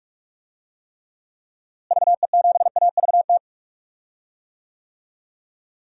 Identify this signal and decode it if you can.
Morse code, heard as a keyed tone, about 46 wpm, 700 Hz. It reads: VE7AVT
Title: DXCC-100 Call Signs - 6 Characters - Repeated 3x - 45wpm